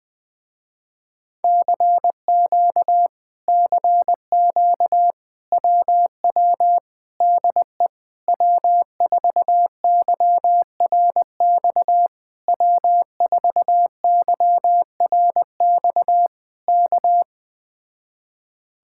Morse code, heard as a keyed tone, about 20 wpm, 700 Hz. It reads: CQ CQ WW DE W4YRX W4YRX K
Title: Speed-Racing Common QSO Phrases - 17wpm